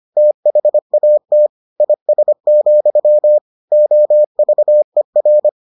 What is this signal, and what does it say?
Morse code, heard as a keyed tone, about 25 wpm, 600 Hz. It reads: THAT IS, OVER